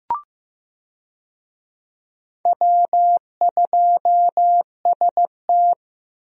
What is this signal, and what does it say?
Morse code, heard as a keyed tone, about 15 wpm, 700 Hz. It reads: W2ST
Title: USA Call Signs - 4 Characters - Repeated 2x - 15wpm